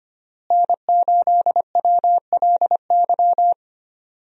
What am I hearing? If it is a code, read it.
Morse code, heard as a keyed tone, about 25 wpm, 700 Hz. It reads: N8WLY
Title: USA Call Signs - 5 Characters - Repeated 3x - 25wpm